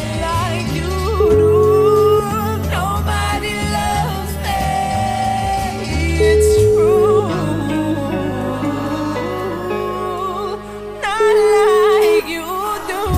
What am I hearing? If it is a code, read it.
Ringback tone (Europe, 425 Hz), with noise.